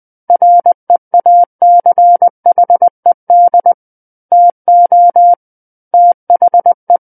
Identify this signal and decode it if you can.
Morse code, heard as a keyed tone, about 20 wpm, 700 Hz. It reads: REACHED TO THE
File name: 20 WPM